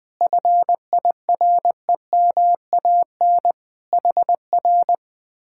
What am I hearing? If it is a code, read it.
Morse code, heard as a keyed tone, about 20 wpm, 700 Hz. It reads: FIREMAN HR
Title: Speed-Racing Common QSO Phrases - 20wpm